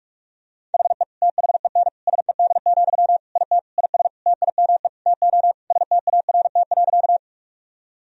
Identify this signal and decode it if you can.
Morse code, heard as a keyed tone, about 46 wpm, 700 Hz. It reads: HE THEN SED, IT IS TIME TO START.
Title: Sentences from Top 300 Words and Abbreviations 45wpm